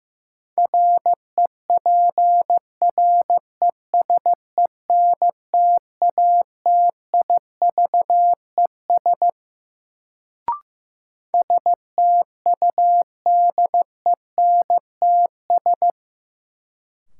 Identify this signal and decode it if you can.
Morse code, heard as a keyed tone, about 15 wpm, 700 Hz. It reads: REPRESENTATIVES STUDENTS
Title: Sets of 5 Words Encore - New Word Drill - 15wpm